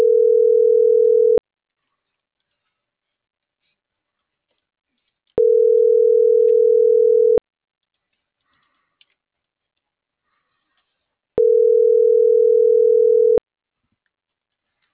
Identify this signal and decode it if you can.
Ringback tone (North America), 440 + 480 Hz.